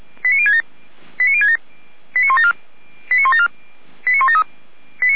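Five-tone selective call (ZVEI), heard as quick runs of five tones, tones of about 80 ms. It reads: 89067 89067 89174 89174 89173
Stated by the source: ZVEI five-tone call received by SDR.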